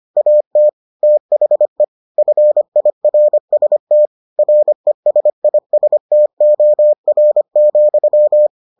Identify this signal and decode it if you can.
Morse code, heard as a keyed tone, about 25 wpm, 600 Hz. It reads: AT THE FIRST RESISTOR,